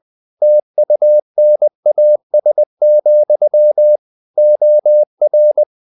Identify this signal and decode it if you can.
Morse code, heard as a keyed tone, about 20 wpm, 600 Hz. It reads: TUNAS, OR
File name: the old man and the sea / the old man and the sea part 1 20wpm